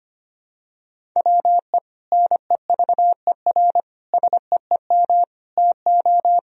Morse code, heard as a keyed tone, about 25 wpm, 700 Hz. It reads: WE NEVER SEEM TO
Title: Sentences from Top 1000 Words - Review Part 1 - 25wpm